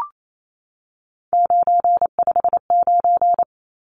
Morse code, heard as a keyed tone, about 28 wpm, 700 Hz. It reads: 959
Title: Numbers - 3-Digit 28wpm